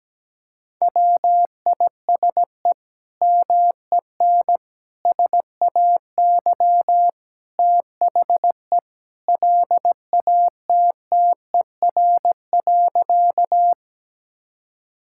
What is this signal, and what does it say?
Morse code, heard as a keyed tone, about 17 wpm, 700 Hz. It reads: WISE MEN SAY THE LATTER.
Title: Sentences from Top 1600 Words 17wpm